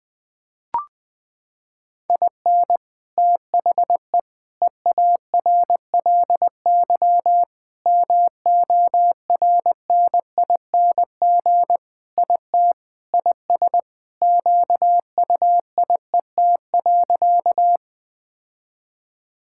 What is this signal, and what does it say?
Morse code, heard as a keyed tone, about 21 wpm, 700 Hz. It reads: IN THE EARLY MORNING IT IS QUIET.
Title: Sentences with 3-Word Phrases - 20wpm